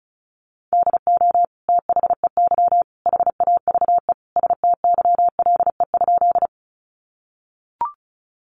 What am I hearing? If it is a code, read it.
Morse code, heard as a keyed tone, about 34 wpm, 700 Hz. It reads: DO THEY HAVE STYLE?
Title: Sentences from Top 800 Words 35wpm